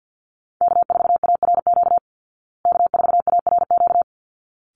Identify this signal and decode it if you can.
Morse code, heard as a keyed tone, about 48 wpm, 700 Hz. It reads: K4ARQ K4ARQ
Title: Call Signs Repeated 3x - 50wpm